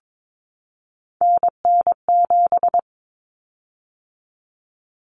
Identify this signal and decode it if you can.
Morse code, heard as a keyed tone, about 22 wpm, 700 Hz. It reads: NN7
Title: Common Cut Numbers - 3-Digit 22wpm